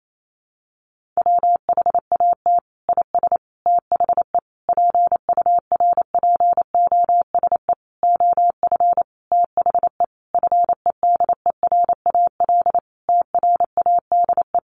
Morse code, heard as a keyed tone, about 28 wpm, 700 Hz. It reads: WHAT IS THE PURPOSE OF THE FEDERAL TRADE
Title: Sentences from Top 1400 Words 28wpm